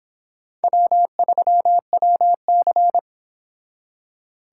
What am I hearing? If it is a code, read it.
Morse code, heard as a keyed tone, about 26 wpm, 700 Hz. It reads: W3WC